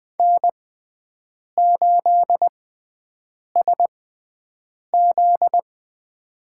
Morse code, heard as a keyed tone, about 20 wpm, spaced as for 7 wpm, 700 Hz. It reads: N8SZ
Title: USA Call Signs - 4 Characters - 6x Character Spacing - 20wpm